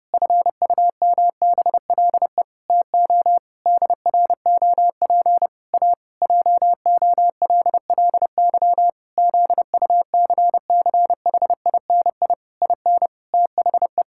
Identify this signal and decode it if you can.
Morse code, heard as a keyed tone, about 30 wpm, 700 Hz. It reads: FUMBLE TO DROP A JOLLY ZUCCHINI IN THE
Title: Pangram Challenge v2 - 30wpm